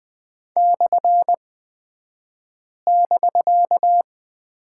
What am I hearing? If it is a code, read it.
Morse code, heard as a keyed tone, about 20 wpm, spaced as for 5 wpm, 700 Hz. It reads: /<BK>
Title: 249 Lesson 40d - BK - Two Characters 20wpm